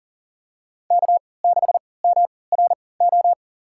Morse code, heard as a keyed tone, about 40 wpm, spaced as for 23 wpm, 700 Hz. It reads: K6MRO
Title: USA Call Signs - 5 Characters - 3x Character Spacing - 40wpm